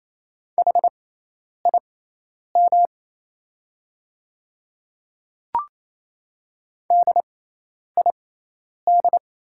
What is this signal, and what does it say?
Morse code, heard as a keyed tone, about 28 wpm, spaced as for 10 wpm, 700 Hz. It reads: HIM DID
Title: 6x Character Spacing - Top 100 Words - 28wpm